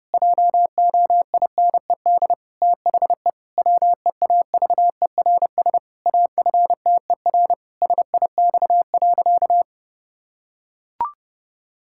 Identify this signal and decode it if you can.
Morse code, heard as a keyed tone, about 30 wpm, 700 Hz. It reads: JOINED THE WEAVERS AFTER SIX.
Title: Pangram Challenge v2 - 30wpm